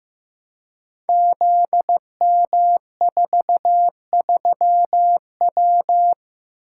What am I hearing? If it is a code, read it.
Morse code, heard as a keyed tone, about 15 wpm, 700 Hz. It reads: ZM43W
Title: Code Groups - 5 Letter:Numbers - 15wpm